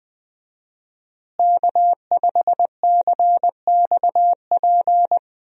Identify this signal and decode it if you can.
Morse code, heard as a keyed tone, about 21 wpm, 700 Hz. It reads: K5CXP
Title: USA Call Signs - 5 Characters - Repeated 3x - 20wpm